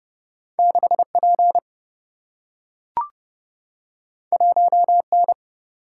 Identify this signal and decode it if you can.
Morse code, heard as a keyed tone, about 30 wpm, 700 Hz. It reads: 6P 1N
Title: Single-Letter Single-Number 30wpm